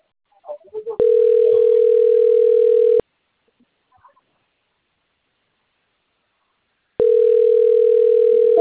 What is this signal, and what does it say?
Ringback tone (North America), 440 + 480 Hz.